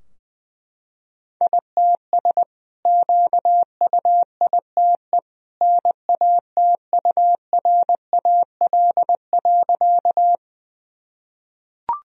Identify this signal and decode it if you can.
Morse code, heard as a keyed tone, about 21 wpm, 700 Hz. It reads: ITS QUITE NATURAL.
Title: Sentences from Top 1000 Words - Review Part 4 - 20wpm